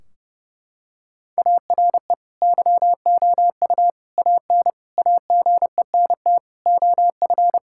Morse code, heard as a keyed tone, about 30 wpm, 700 Hz. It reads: ARE YOU AN AGENT OF
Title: Sentences from Top 1600 Words 30wpm